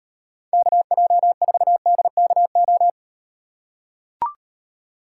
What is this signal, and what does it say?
Morse code, heard as a keyed tone, about 38 wpm, 700 Hz. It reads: KJ4DKO